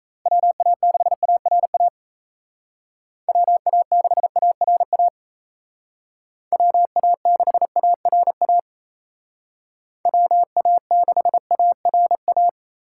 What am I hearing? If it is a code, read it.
Morse code, heard as a keyed tone, about 36 wpm, 700 Hz. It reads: WA6ARA WA6ARA WA6ARA WA6ARA